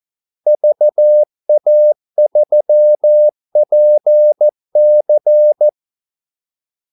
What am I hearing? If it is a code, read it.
Morse code, heard as a keyed tone, about 14 wpm, 600 Hz. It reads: VA3PC
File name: Callsigns-14wpm